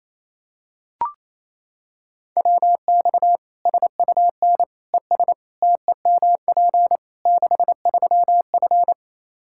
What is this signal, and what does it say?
Morse code, heard as a keyed tone, about 27 wpm, 700 Hz. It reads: WX SUN ES TEMP 63F
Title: Practice QSO - Exchange 2 - Info - 28wpm